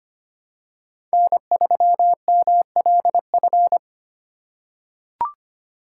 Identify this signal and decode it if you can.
Morse code, heard as a keyed tone, about 25 wpm, 700 Hz. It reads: N3MLF
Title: USA Call Signs - 5 Characters - 2x Character Spacing - 25wpm